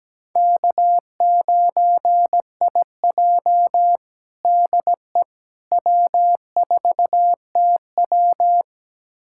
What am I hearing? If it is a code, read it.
Morse code, heard as a keyed tone, about 17 wpm, 700 Hz. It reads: K9IJ DE W4TW
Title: USA Callsign DE Callsign - 4 Characters - 17wpm